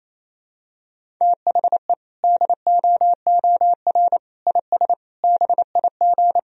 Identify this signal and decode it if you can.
Morse code, heard as a keyed tone, about 28 wpm, 700 Hz. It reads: THE DOOR IS BIG
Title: Sentences from Top 1000 Words - Review Part 1 - 28wpm